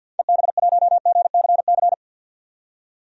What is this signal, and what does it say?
Morse code, heard as a keyed tone, about 47 wpm, 700 Hz. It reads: EB1GCC
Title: DXCC-100 Call Signs - 6 Characters - 50wpm